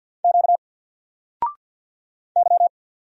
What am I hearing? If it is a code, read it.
Morse code, heard as a keyed tone, about 50 wpm, 700 Hz. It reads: Q Y
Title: Single Letter-Number 50wpm